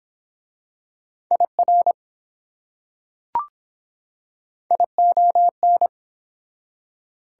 Morse code, heard as a keyed tone, about 22 wpm, 700 Hz. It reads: IR ION